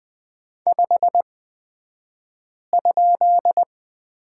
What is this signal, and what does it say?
Morse code, heard as a keyed tone, about 20 wpm, spaced as for 5 wpm, 700 Hz. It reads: 5?
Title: 235 Lesson 38d - Letter q - Two Characters 20wpm